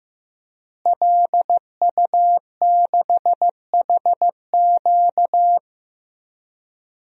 Morse code, heard as a keyed tone, about 15 wpm, 700 Hz. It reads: LU6HQ
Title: DXCC-100 Call Signs - 5 Characters - Repeated 2x - 15wpm